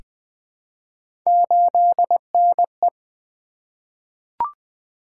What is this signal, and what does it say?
Morse code, heard as a keyed tone, about 20 wpm, 700 Hz. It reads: 8NE
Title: Three Letter-Number Combinations - 20wpm